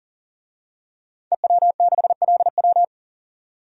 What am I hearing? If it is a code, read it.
Morse code, heard as a keyed tone, about 40 wpm, 700 Hz. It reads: EW6LW
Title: DXCC-100 Call Signs - 5 Characters - 40wpm